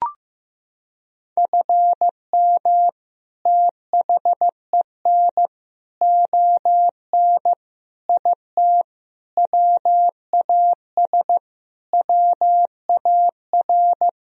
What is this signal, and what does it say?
Morse code, heard as a keyed tone, about 15 wpm, 700 Hz. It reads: FM THEN ON IT WAS WAR
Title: Sentences from Top 400 Words and Abbreviations 15wpm